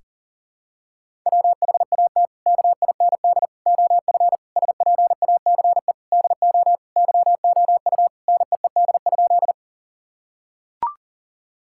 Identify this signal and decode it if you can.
Morse code, heard as a keyed tone, about 40 wpm, 700 Hz. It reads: WHAT KIND OF SPACE DO YOU NEED?
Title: Sentences from Top 1000 Words - Review Part 5 - 40wpm